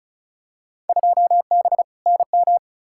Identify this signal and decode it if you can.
Morse code, heard as a keyed tone, about 34 wpm, 700 Hz. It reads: 2B NM
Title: Field Day Exchanges Encore - 35wpm